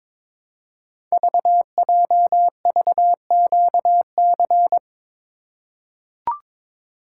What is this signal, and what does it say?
Morse code, heard as a keyed tone, about 23 wpm, 700 Hz. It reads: VJVQC
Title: Code Groups - 5 Letters - 22wpm